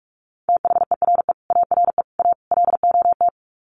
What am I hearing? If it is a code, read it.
Morse code, heard as a keyed tone, about 44 wpm, 700 Hz. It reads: THERE ARE A LOT
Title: Sets of 4 Words 45wpm